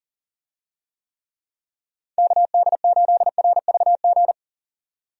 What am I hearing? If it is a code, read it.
Morse code, heard as a keyed tone, about 40 wpm, 700 Hz. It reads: KD8RVG
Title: USA Call Signs - 6 Characters - 40wpm